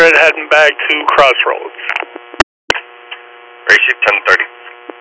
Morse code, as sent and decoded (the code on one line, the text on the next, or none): none